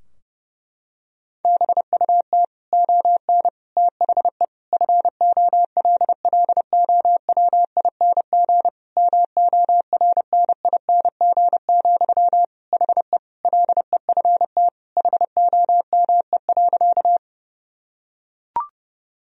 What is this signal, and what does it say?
Morse code, heard as a keyed tone, about 32 wpm, 700 Hz. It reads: BUT ON THE FOLLOWING MORNING, HE LEFT HOME.
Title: Sentences from Top 1000 Words - Review Part 4 - 30wpm